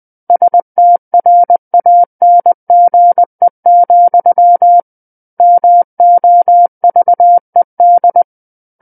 Morse code, heard as a keyed tone, about 20 wpm, 700 Hz. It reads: STRANGE, MOVED
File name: 20 WPM